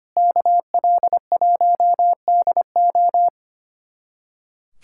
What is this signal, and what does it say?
Morse code, heard as a keyed tone, about 25 wpm, 700 Hz. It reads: KL1DO
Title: USA Callsign DE Callsign - 5 Characters - 25wpm